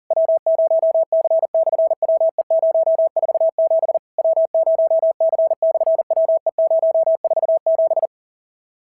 Morse code, heard as a keyed tone, about 40 wpm, 650 Hz. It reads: W0C/WE047 W0C/WE047
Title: SOTA QSO Practice 40wpm